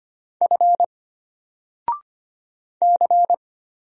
Morse code, heard as a keyed tone, about 25 wpm, 700 Hz. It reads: F C
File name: Single-Letter-Number-25wpm